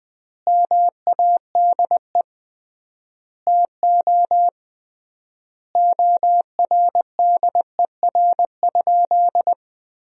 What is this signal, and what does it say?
Morse code, heard as a keyed tone, about 20 wpm, 700 Hz. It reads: MADE TO ORDER?
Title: 3x Word-Spacing - Sentences from Top 400 Words - 20wpm